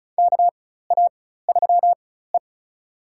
Morse code, heard as a keyed tone, about 36 wpm, spaced as for 16 wpm, 700 Hz. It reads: KA3E
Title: USA Call Signs - 4 Characters - 4x Character Spacing - 35wpm